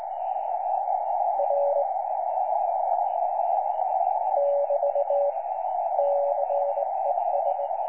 Morse code, heard as a keyed tone, about 20 wpm, 570 Hz, with noise. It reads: R <BT> CES
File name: 12 39Z 18088.2kHz utah aa7fv k9tjl